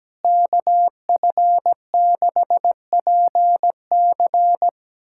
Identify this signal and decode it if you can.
Morse code, heard as a keyed tone, about 17 wpm, 700 Hz. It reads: KF6PC
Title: USA Callsign DE Callsign - 5 Characters - 17wpm